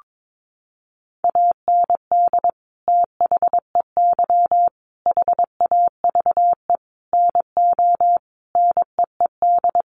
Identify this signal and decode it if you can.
Morse code, heard as a keyed tone, about 22 wpm, 700 Hz. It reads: AND THEY HAVE NO NEED
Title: Sentences from Top 1400 Words 22wpm